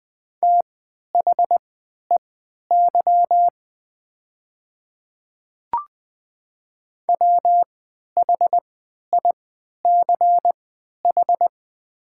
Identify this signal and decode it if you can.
Morse code, heard as a keyed tone, about 20 wpm, spaced as for 11 wpm, 700 Hz. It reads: THEY WHICH
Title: 3x Character Spacing - Top 100 Words - 20wpm